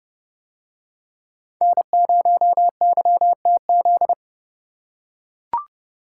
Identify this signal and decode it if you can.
Morse code, heard as a keyed tone, about 29 wpm, 700 Hz. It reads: N0YTZ